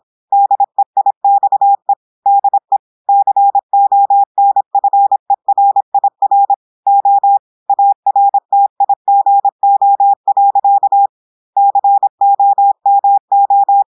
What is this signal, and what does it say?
Morse code, heard as a keyed tone, about 26 wpm, 800 Hz. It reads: DEIXE DE CONFERIR O ARTIGO. COMO
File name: portugais texte en clair 2 vitesse 26 mots minutes